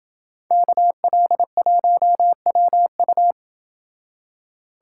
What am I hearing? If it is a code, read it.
Morse code, heard as a keyed tone, about 27 wpm, 700 Hz. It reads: KL1WU